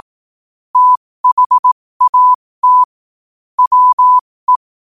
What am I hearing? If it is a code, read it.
Morse code, heard as a keyed tone, about 18 wpm, spaced as for 15 wpm, 1000 Hz. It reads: THAT WE
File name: qotdfiles / 15wpm